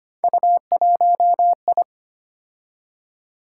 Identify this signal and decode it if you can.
Morse code, heard as a keyed tone, about 25 wpm, 700 Hz. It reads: U1I
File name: Three Letter-Number Combinations - 25wpm